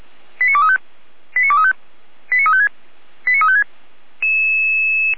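Five-tone selective call (ZVEI), heard as quick runs of five tones, tones of about 70 ms. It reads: 89235 89235 89356 89356
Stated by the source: ZVEI five-tone selective call 89126 received by SDR.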